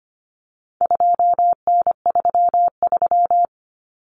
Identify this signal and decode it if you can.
Morse code, heard as a keyed tone, about 25 wpm, 700 Hz. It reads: 2N33
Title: Common Cut Numbers - 4-Digit 25wpm